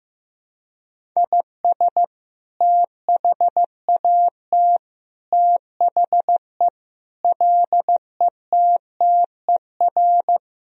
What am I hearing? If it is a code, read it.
Morse code, heard as a keyed tone, about 15 wpm, 700 Hz. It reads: IS THAT THE LETTER
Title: Sentences from Top 1000 Words - Review Part 2 - 15wpm